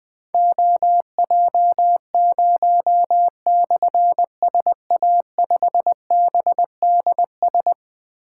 Morse code, heard as a keyed tone, about 20 wpm, 700 Hz. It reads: OJ0/SA5BDS
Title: Difficult Call Signs 20wpm